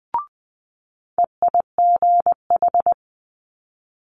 Morse code, heard as a keyed tone, about 20 wpm, 700 Hz. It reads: EIGH
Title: Multi-letter Phonemes Quick Fire 20wpm